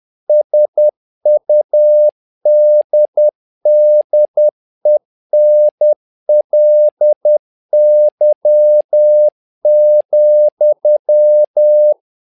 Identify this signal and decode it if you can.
Morse code, heard as a keyed tone, about 10 wpm, 600 Hz. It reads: SUDDENLY,
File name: learning center / tower trouble / tower trouble 10wpm